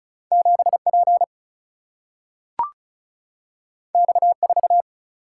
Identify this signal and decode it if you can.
Morse code, heard as a keyed tone, about 36 wpm, 700 Hz. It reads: 7P X4
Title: Single-Letter Single-Number 35wpm